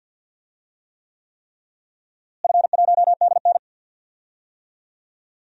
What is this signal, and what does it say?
Morse code, heard as a keyed tone, about 51 wpm, 700 Hz. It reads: F1DN
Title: DXCC-100 Call Signs - 4 Characters - Repeated 3x - 50wpm